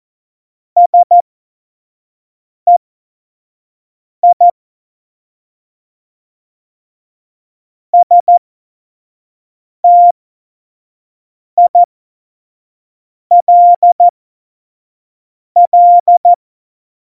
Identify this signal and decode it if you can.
Morse code, heard as a keyed tone, about 14 wpm, spaced as for 5 wpm, 700 Hz. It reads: SEI STILL
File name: Morsekurs / Witze / Witz 11. Juni 2019, 25 F70